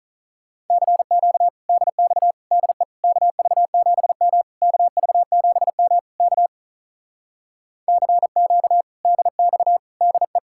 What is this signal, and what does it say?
Morse code, heard as a keyed tone, about 46 wpm, 700 Hz. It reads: CQ DX DE KV7M KV7M K CQ DX DE